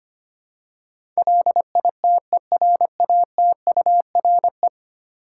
Morse code, heard as a keyed tone, about 25 wpm, 700 Hz. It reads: LITERATURE
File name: 10 Letter Words - First 3 Letters Given - 25wpm